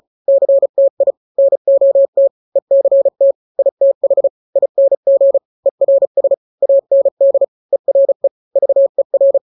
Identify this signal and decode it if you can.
Morse code, heard as a keyed tone, about 34 wpm, 550 Hz. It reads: CTI NOT ECT ITH ING ERS AND ERE VER
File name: 3-grams 35 30 WPM